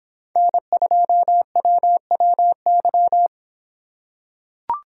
Morse code, heard as a keyed tone, about 25 wpm, 700 Hz. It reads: N2WWY